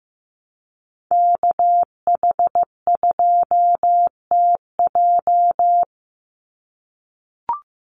Morse code, heard as a keyed tone, about 15 wpm, 700 Hz. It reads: KH2TJ
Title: NA SOTA Activator Callsigns 15wpm